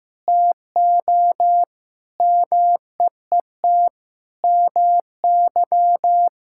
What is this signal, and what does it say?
Morse code, heard as a keyed tone, about 15 wpm, 700 Hz. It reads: TO MEET MY
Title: Sentences from Top 1000 Words - Review Part 6 - 15wpm